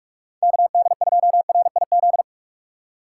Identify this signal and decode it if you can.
Morse code, heard as a keyed tone, about 46 wpm, 700 Hz. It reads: KD2RIZ